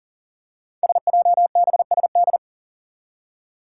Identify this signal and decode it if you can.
Morse code, heard as a keyed tone, about 40 wpm, 700 Hz. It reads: SJBSD
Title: Five Challenging Letter-Number Combinations 40wpm